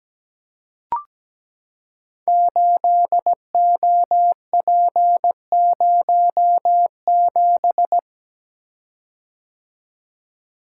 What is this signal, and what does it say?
Morse code, heard as a keyed tone, about 17 wpm, 700 Hz. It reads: 8OP07
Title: Code Groups - 5 Letter:Numbers - 17wpm